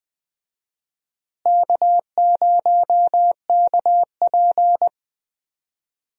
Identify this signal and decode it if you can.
Morse code, heard as a keyed tone, about 20 wpm, 700 Hz. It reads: K0KP
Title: USA Call Signs - 4 Characters - Repeated 3x - 20wpm